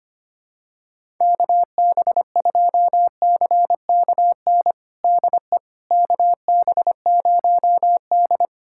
Morse code, heard as a keyed tone, about 25 wpm, 700 Hz. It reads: KB2CKN DE KB0D
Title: USA Callsign DE Callsign 25wpm